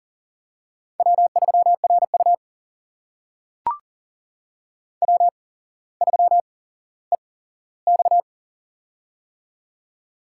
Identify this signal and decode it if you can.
Morse code, heard as a keyed tone, about 40 wpm, spaced as for 17 wpm, 700 Hz. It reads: W3RU W3EX
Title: USA Call Signs - 8x Character Spacing - 40wpm